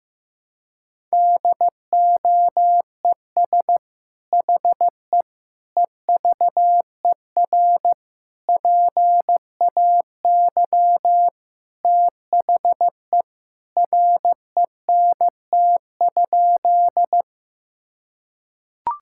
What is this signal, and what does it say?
Morse code, heard as a keyed tone, about 15 wpm, 700 Hz. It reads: DOES HE EVER PAY THE RENT?
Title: Sentences from Top 1000 Words - Review Part 5 - 15wpm